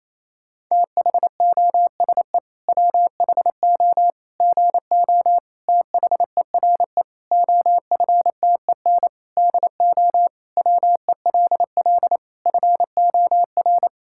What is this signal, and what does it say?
Morse code, heard as a keyed tone, about 27 wpm, 700 Hz. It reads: THOSE WHO GO THERE OFTEN DO WELL FOR
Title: Sentences from Top 1000 Words - Review Part 5 - 28wpm